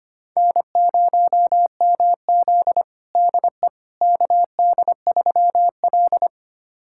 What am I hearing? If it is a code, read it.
Morse code, heard as a keyed tone, about 25 wpm, 700 Hz. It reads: N0MZ DE KD3L
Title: USA Callsign DE Callsign - 4 Characters - 25wpm